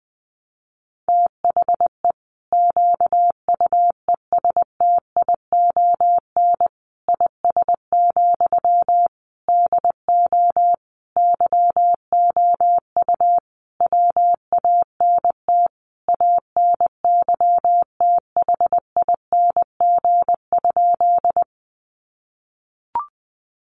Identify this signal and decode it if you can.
Morse code, heard as a keyed tone, about 20 wpm, 700 Hz. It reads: THE QUESTION IS, DO YOU WANT ANYTHING?
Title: Sentences from Top 1000 Words 20wpm